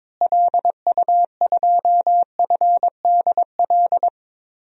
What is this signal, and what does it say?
Morse code, heard as a keyed tone, about 22 wpm, 700 Hz. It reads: LU2FDL
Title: DXCC-100 Call Signs - 6 Characters - Repeated 3x - 22wpm